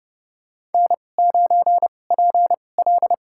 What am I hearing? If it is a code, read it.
Morse code, heard as a keyed tone, about 30 wpm, spaced as for 22 wpm, 700 Hz. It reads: N9PL
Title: USA Call Signs - 4 Characters - 2x Character Spacing - 30wpm